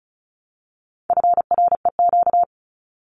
Morse code, heard as a keyed tone, about 34 wpm, 700 Hz. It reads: FREQ
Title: Abbreviations Repeated 3x - 35wpm